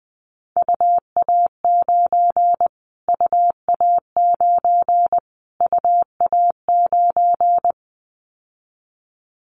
Morse code, heard as a keyed tone, about 20 wpm, 700 Hz. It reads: UA9 UA9 UA9
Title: Callsign Prefixes for Top 25 DXCC - Repeated 3x - 20wpm